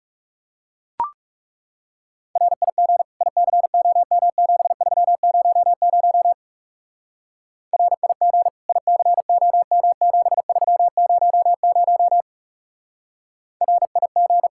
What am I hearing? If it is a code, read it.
Morse code, heard as a keyed tone, about 37 wpm, 700 Hz. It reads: RIG ICOM7300 RIG ICOM7300 RIG
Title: Speed-Racing Common QSO Phrases Encore - 30wpm